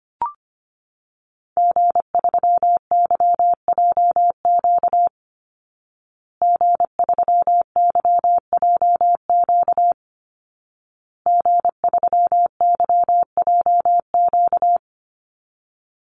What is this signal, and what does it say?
Morse code, heard as a keyed tone, about 25 wpm, 700 Hz. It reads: G3YJQ G3YJQ G3YJQ
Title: Call Signs Repeated 2x - 25wpm